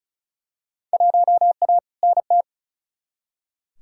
Morse code, heard as a keyed tone, about 36 wpm, 700 Hz. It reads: 1A NT
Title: Field Day Exchanges Encore - 35wpm